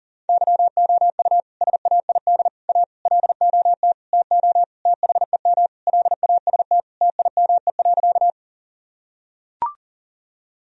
Morse code, heard as a keyed tone, about 40 wpm, 700 Hz. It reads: YOU SAID A LOT TO THEM LAST TIME.
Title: Sentences from Top 1000 Words - Review Part 3 - 40wpm